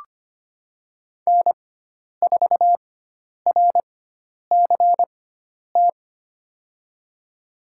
Morse code, heard as a keyed tone, about 25 wpm, spaced as for 10 wpm, 700 Hz. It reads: N4RCT
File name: Lesson 16g - Letter c - Call Signs 25wpm